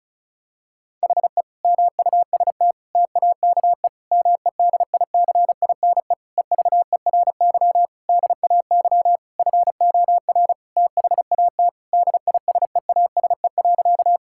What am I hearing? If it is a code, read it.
Morse code, heard as a keyed tone, about 34 wpm, 700 Hz. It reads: HE MUST TAKE MEDICINE EVERY DAY FOR THAT DISEASE.
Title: Sentences from Top 1000 Words - Review Part 6 - 35wpm